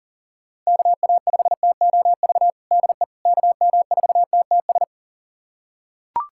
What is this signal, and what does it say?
Morse code, heard as a keyed tone, about 40 wpm, 700 Hz. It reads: KA5TOV DE KM4TTS
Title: USA Callsign DE Callsign - 6 Characters - 40wpm